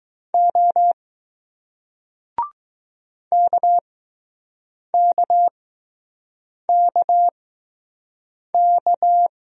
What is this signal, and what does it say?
Morse code, heard as a keyed tone, about 20 wpm, spaced as for 6 wpm, 700 Hz. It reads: O KKKK